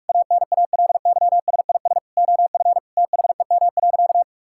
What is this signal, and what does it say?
Morse code, heard as a keyed tone, about 46 wpm, 690 Hz. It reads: ANALYSIS OF THEM.
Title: Sentences from Top 1000 Words - Review Part 5 - 45wpm